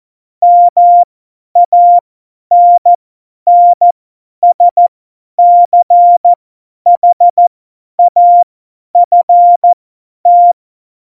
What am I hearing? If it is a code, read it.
Morse code, heard as a keyed tone, about 14 wpm, spaced as for 10 wpm, 700 Hz. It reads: MANNSCHAFT
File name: Morsekurs / Witze / Witz 05. Januar 2021, 50 F70